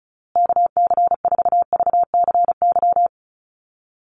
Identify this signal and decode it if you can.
Morse code, heard as a keyed tone, about 34 wpm, 700 Hz. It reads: KC4VCY
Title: USA Call Signs Repeated 2x 35wpm